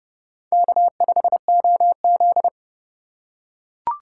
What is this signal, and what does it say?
Morse code, heard as a keyed tone, about 27 wpm, 700 Hz. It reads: K5OZ